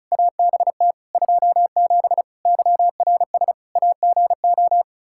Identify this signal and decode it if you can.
Morse code, heard as a keyed tone, about 36 wpm, 700 Hz. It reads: ABT 27 YRS AGO
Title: K1USN SST QSO Practice 35wpm